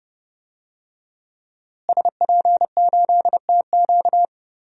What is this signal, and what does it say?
Morse code, heard as a keyed tone, about 30 wpm, 700 Hz. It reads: SP8TQ
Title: DXCC-100 Call Signs - 5 Characters - Repeated 2x - 30wpm